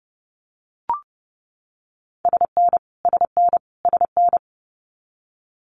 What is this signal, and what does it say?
Morse code, heard as a keyed tone, about 30 wpm, 700 Hz. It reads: SN SN SN
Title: Multi-letter Phonemes Repeated 3x 30wpm